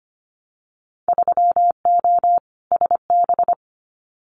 Morse code, heard as a keyed tone, about 25 wpm, 700 Hz. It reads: 3O SB
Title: Winter Field Day Exchanges Encore 25wpm